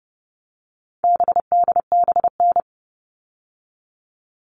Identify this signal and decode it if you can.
Morse code, heard as a keyed tone, about 29 wpm, 700 Hz. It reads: BDBN
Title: All Cut Numbers - 4-Digit 30wpm